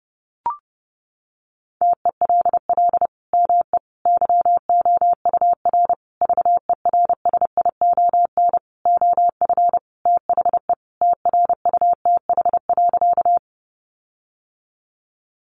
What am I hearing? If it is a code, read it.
Morse code, heard as a keyed tone, about 30 wpm, 700 Hz. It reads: TELL ME YOUR VERSION OF THE TRUTH.
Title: Sentences from Top 1400 Words 30wpm